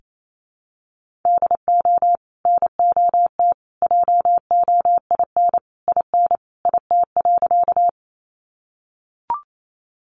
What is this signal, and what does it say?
Morse code, heard as a keyed tone, about 28 wpm, 700 Hz. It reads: DO NOT JOIN IN IT.
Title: Sentences from Top 1400 Words 28wpm